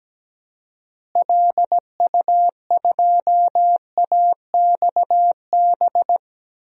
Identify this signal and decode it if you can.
Morse code, heard as a keyed tone, about 17 wpm, 700 Hz. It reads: LU2AXB
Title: DXCC-100 Call Signs - 6 Characters - Repeated 3x - 17wpm